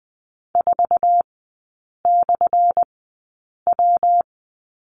Morse code, heard as a keyed tone, about 20 wpm, spaced as for 8 wpm, 700 Hz. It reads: 4/W
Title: 191 Lesson 31e - Letter v - Three Characters 20wpm